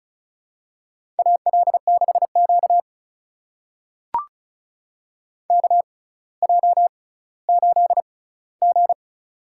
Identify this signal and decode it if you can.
Morse code, heard as a keyed tone, about 36 wpm, spaced as for 18 wpm, 700 Hz. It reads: AL6Q KJ8G
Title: USA Call Signs - 4 Characters - 6x Character Spacing - 35wpm